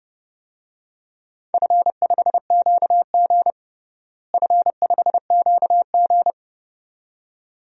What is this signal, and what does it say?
Morse code, heard as a keyed tone, about 30 wpm, 700 Hz. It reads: F5QG F5QG
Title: DXCC-100 Call Signs - 4 Characters - Repeated 2x - 30wpm